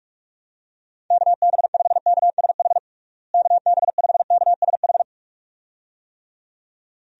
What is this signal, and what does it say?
Morse code, heard as a keyed tone, about 46 wpm, 700 Hz. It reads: KB5KSH KB5KSH
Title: USA Call Signs - 6 Characters - Repeated 2x - 45wpm